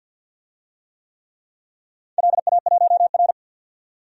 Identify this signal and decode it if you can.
Morse code, heard as a keyed tone, about 51 wpm, 700 Hz. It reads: LA1R
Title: DXCC-100 Call Signs - 4 Characters - 50wpm